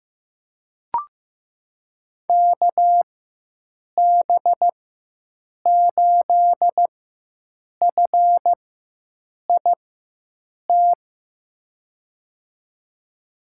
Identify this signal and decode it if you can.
Morse code, heard as a keyed tone, about 15 wpm, spaced as for 7 wpm, 700 Hz. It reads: KB8FIT
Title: USA Call Signs - 4x Character Spacing - 15wpm